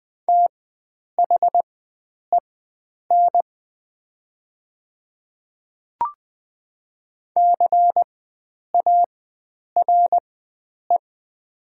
Morse code, heard as a keyed tone, about 20 wpm, spaced as for 9 wpm, 700 Hz. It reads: THEN CARE
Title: 4x Character Spacing - Top 300 Words - 20wpm